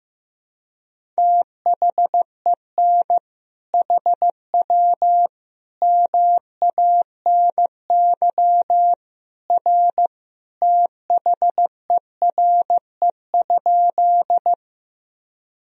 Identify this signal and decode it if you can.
Morse code, heard as a keyed tone, about 15 wpm, 700 Hz. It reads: THEN HW MANY R THERE?
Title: Sentences from Top 100 Words and Abbreviations 15wpm